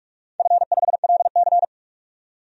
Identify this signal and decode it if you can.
Morse code, heard as a keyed tone, about 46 wpm, 700 Hz. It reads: F5LC
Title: DXCC-100 Call Signs - 4 Characters - 45wpm